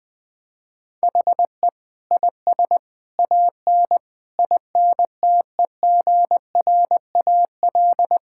Morse code, heard as a keyed tone, about 21 wpm, 700 Hz. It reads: HE IS AN INTEGRAL
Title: Sentences with 5-Word Phrases - Spoken Beginning - 20wpm